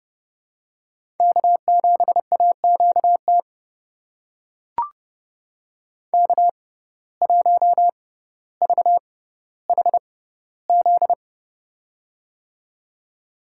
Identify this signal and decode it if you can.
Morse code, heard as a keyed tone, about 30 wpm, spaced as for 15 wpm, 700 Hz. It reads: K7AQT K1VHZ
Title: USA Call Signs - 5 Characters - 6x Character Spacing - 30wpm